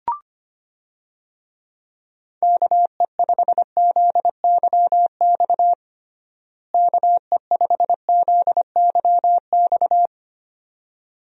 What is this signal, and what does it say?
Morse code, heard as a keyed tone, about 25 wpm, 700 Hz. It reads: KE5ZYX KE5ZYX
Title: USA Call Signs - 6 Characters - Repeated 2x - 25wpm